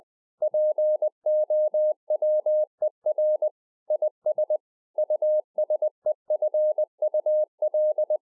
Morse code, heard as a keyed tone, about 20 wpm, 610 Hz. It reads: POWER IS USEFUL
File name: 2011-04-mp3 / A SMALL TRANCEIVER 3 20WPM